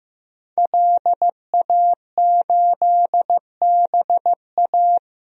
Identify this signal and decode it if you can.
Morse code, heard as a keyed tone, about 15 wpm, 700 Hz. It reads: LA8BA
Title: DXCC-100 Call Signs - 5 Characters - Repeated 2x - 15wpm